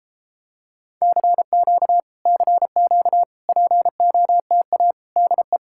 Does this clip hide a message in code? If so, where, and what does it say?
Yes: Morse code from 1.0 to 5.6 s, heard as a keyed tone, about 33 wpm, 700 Hz. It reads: CQ CQ POTA DE